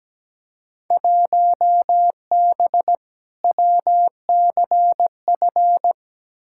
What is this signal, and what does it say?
Morse code, heard as a keyed tone, about 17 wpm, 700 Hz. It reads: 1B WCF
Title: Field Day Exchanges Encore - 17wpm